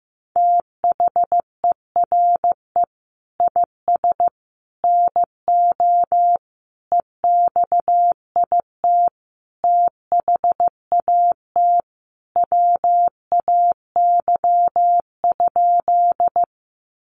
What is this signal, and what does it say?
Morse code, heard as a keyed tone, about 15 wpm, 700 Hz. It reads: THERE IS NO EXIT THAT WAY?
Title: Sentences from Top 1100 Words 15wpm